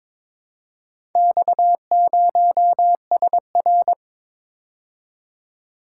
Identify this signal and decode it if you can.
Morse code, heard as a keyed tone, about 21 wpm, 700 Hz. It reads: X0SR
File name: practice / 22wpm / Four Letter-Number Combinations - 22wpm